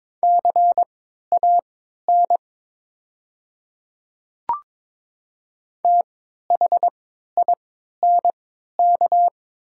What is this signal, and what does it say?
Morse code, heard as a keyed tone, about 22 wpm, spaced as for 13 wpm, 700 Hz. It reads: CAN THINK
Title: 3x Character Spacing - Top 200 Words - 22wpm